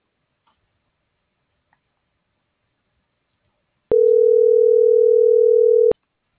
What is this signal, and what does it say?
Ringback tone (North America), 440 + 480 Hz.